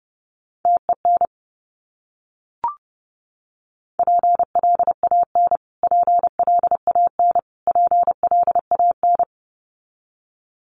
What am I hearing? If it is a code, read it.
Morse code, heard as a keyed tone, about 30 wpm, 700 Hz. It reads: TEN PLAN PLAN PLAN
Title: Top 500 Words Repeated 30wpm